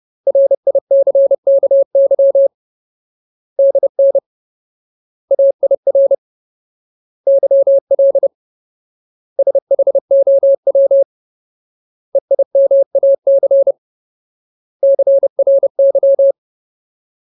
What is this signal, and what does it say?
Morse code, heard as a keyed tone, about 30 wpm, 550 Hz. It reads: RICKY DN AIR YL SHOW EIMAC CRY
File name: Ultimate Word List / 30WPM Ultimate